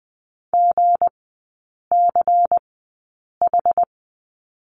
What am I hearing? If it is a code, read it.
Morse code, heard as a keyed tone, about 19 wpm, spaced as for 8 wpm, 690 Hz. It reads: GCH
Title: 166 Lesson 27e - Number 7 - Three Characters 20wpm